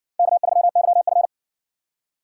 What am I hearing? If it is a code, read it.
Morse code, heard as a keyed tone, about 58 wpm, 700 Hz. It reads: K3YF